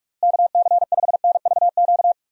Morse code, heard as a keyed tone, about 41 wpm, 700 Hz. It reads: KC5NVQ
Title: USA Call Signs - 6 Characters - Repeated 2x - 45wpm